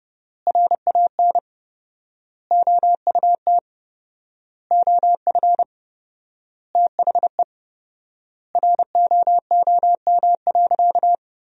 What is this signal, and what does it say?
Morse code, heard as a keyed tone, about 30 wpm, 700 Hz. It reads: RAN OUT OF THE ROOM.
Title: 4x Word-Spacing - Sentences from Top 500 Words - 30wpm